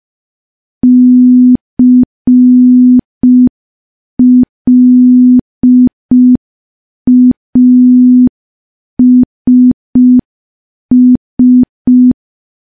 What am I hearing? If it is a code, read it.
Morse code, heard as a keyed tone, about 5 wpm, 250 Hz. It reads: CLASS
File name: morse / morse-training-05WPM